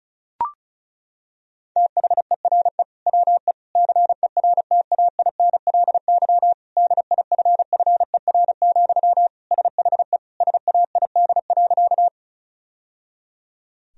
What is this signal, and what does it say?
Morse code, heard as a keyed tone, about 36 wpm, 700 Hz. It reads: THERE WE CERTAINLY DIFFER, SHE SAID.
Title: Sentences from Top 1000 Words - Review Part 1 - 35wpm